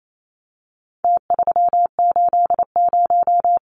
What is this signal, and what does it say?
Morse code, heard as a keyed tone, about 28 wpm, 700 Hz. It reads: T380
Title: Numbers - Contest Style - 4-Digit 28wpm